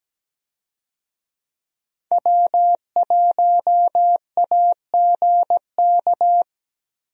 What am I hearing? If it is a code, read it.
Morse code, heard as a keyed tone, about 17 wpm, 700 Hz. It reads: W1AGK
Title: USA Call Signs - 5 Characters - Repeated 2x - 17wpm